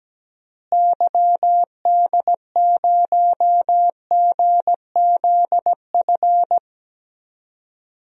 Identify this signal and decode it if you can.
Morse code, heard as a keyed tone, about 17 wpm, 700 Hz. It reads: YD0GZF
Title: DXCC-100 Call Signs - 6 Characters - Repeated 3x - 17wpm